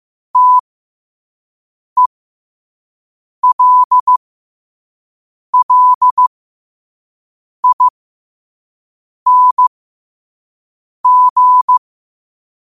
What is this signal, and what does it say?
Morse code, heard as a keyed tone, about 15 wpm, spaced as for 5 wpm, 1000 Hz. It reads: TELLING
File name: qotdfiles / 5wpm